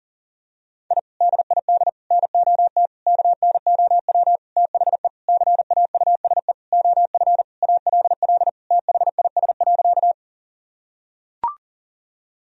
Morse code, heard as a keyed tone, about 40 wpm, 700 Hz. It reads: I DID NOT KNOW THE CAUSE OF ALL THIS.
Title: Sentences from Top 1000 Words - Review Part 1 - 40wpm